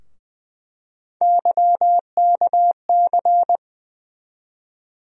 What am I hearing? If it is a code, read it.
Morse code, heard as a keyed tone, about 20 wpm, 700 Hz. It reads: YKC
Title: USA Call Sign Postfixes - 3 Letters 20wpm